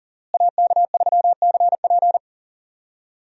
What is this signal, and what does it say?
Morse code, heard as a keyed tone, about 40 wpm, 700 Hz. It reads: AK3CP
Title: USA Call Signs - 5 Characters - Repeated 2x - 40wpm